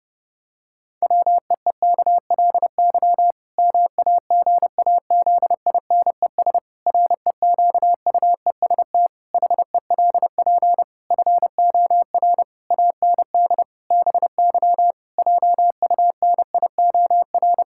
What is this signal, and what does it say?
Morse code, heard as a keyed tone, about 32 wpm, 700 Hz. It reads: WEEKLY MAGAZINES REQUEST HELP FOR AND BY JUNIOR
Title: Pangram Challenge v2 - 30wpm